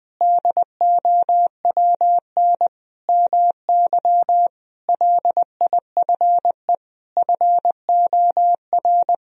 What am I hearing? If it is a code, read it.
Morse code, heard as a keyed tone, about 20 wpm, 700 Hz. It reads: DOWN MY LIFE FOR
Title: Sentences from Top 1000 Words - Review Part 2 - 20wpm